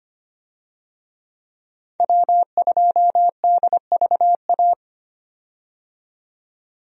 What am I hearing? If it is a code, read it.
Morse code, heard as a keyed tone, about 25 wpm, 700 Hz. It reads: W2DVA
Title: USA Call Signs - 5 Characters - 25wpm